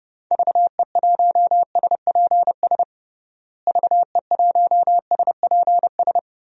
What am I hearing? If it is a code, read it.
Morse code, heard as a keyed tone, about 30 wpm, 700 Hz. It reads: VE1SPS VE1SPS
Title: DXCC-100 Call Signs - 6 Characters - Repeated 2x - 30wpm